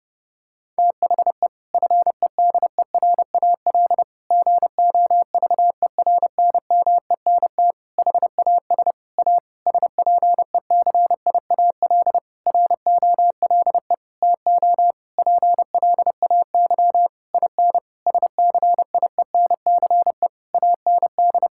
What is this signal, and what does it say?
Morse code, heard as a keyed tone, about 30 wpm, 700 Hz. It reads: THE FEDERAL GOVERNMENT HAS A SPECIAL ROLE TO PLAY IN SCIENCE AND
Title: Sentences from Top 1000 Words - Review Part 6 - 30wpm